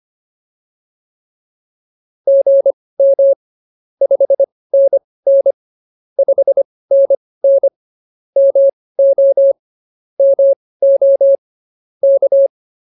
Morse code, heard as a keyed tone, about 24 wpm, spaced as for 18 wpm, 550 Hz. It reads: GM 5NN 5NN MO MO K
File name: POTA201 18